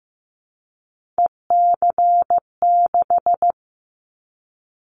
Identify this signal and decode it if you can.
Morse code, heard as a keyed tone, about 15 wpm, 700 Hz. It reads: EC6
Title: Callsign Prefixes for Top 100 DXCC - 15wpm